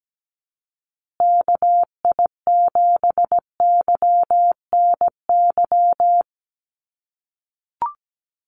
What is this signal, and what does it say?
Morse code, heard as a keyed tone, about 17 wpm, 700 Hz. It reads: KI7YNY
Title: USA Call Signs Repeated 3x 17wpm